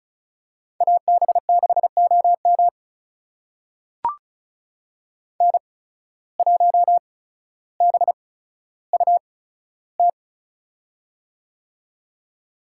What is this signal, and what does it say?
Morse code, heard as a keyed tone, about 36 wpm, spaced as for 15 wpm, 690 Hz. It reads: AB6OM N1BUT
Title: USA Call Signs - 5 Characters - 8x Character Spacing - 35wpm